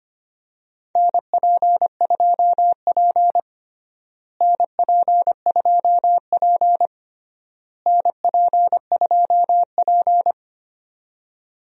Morse code, heard as a keyed tone, about 25 wpm, 700 Hz. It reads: NP2P NP2P NP2P
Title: USA Call Signs - 4 Characters - Repeated 3x - 25wpm